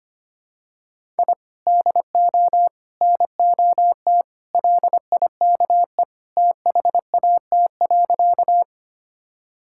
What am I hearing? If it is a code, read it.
Morse code, heard as a keyed tone, about 25 wpm, 700 Hz. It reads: I DO NOT LIKE THAT.
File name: practice / 25wpm / Sentences with 4-Word Phrases - 25wpm